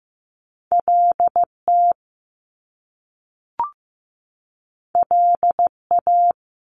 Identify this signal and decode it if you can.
Morse code, heard as a keyed tone, about 15 wpm, 700 Hz. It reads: LT LA
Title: Top Two Letter Combinations in Words 15wpm